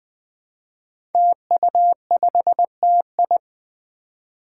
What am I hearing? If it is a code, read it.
Morse code, heard as a keyed tone, about 20 wpm, 700 Hz. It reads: TU5TI
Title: Five Challenging Letter-Number Combinations 20wpm